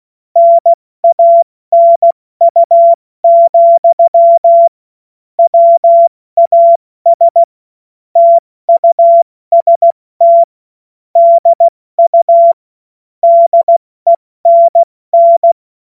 Morse code, heard as a keyed tone, about 16 wpm, spaced as for 14 wpm, 680 Hz. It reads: NANU, WAS TUST DU DENN
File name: Morsekurs / Witze / Witz 17 Juni Mai 2025 70 F80